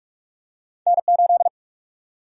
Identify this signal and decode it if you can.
Morse code, heard as a keyed tone, about 46 wpm, 700 Hz. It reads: N8
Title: USA Call Sign Prefixes - 45wpm